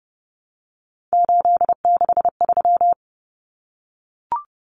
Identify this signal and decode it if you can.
Morse code, heard as a keyed tone, about 30 wpm, 700 Hz. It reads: 863
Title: Common Cut Numbers - 3-Digit 30wpm